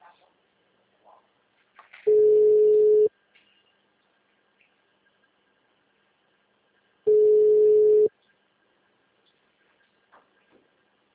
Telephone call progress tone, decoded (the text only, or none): ringback tone